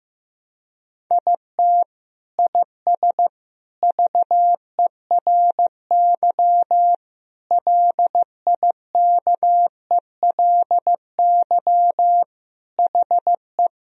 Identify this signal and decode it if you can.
Morse code, heard as a keyed tone, about 15 wpm, 700 Hz. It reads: IT IS VERY LIKELY HE
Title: Sentences from Top 1000 Words - Review Part 4 - 15wpm